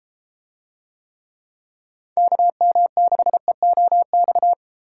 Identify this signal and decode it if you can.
Morse code, heard as a keyed tone, about 32 wpm, 700 Hz. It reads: KM6EOX